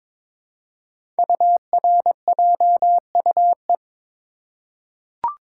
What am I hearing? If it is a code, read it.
Morse code, heard as a keyed tone, about 22 wpm, 700 Hz. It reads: URJUE
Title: Code Groups - 5 Letters - 22wpm